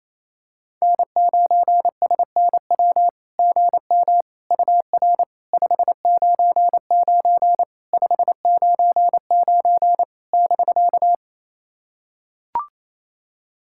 Morse code, heard as a keyed tone, about 28 wpm, 700 Hz. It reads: N9SNW GM UR 599 599 <BK>
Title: SOTA QSO Practice 28wpm